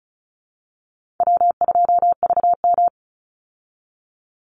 Morse code, heard as a keyed tone, about 34 wpm, 700 Hz. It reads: W2VM
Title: Easy Call Signs Repeated 3x - 35wpm